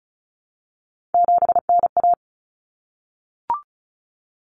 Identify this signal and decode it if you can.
Morse code, heard as a keyed tone, about 34 wpm, 700 Hz. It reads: 7NA
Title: Common Cut Numbers - 3-Digit 35wpm